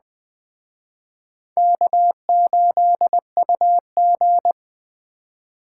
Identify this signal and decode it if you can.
Morse code, heard as a keyed tone, about 21 wpm, 700 Hz. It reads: K8UG